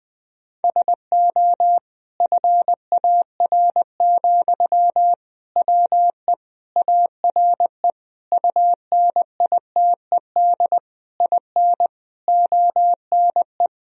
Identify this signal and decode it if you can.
Morse code, heard as a keyed tone, about 20 wpm, 700 Hz. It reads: SO FAR, WE ARE UNITED IN ONE
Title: Sentences from Top 1000 Words - Review Part 4 - 20wpm